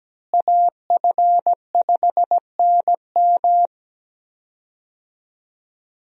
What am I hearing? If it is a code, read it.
Morse code, heard as a keyed tone, about 17 wpm, 700 Hz. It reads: AF5NM
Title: USA Call Signs - 5 Characters - Repeated 2x - 17wpm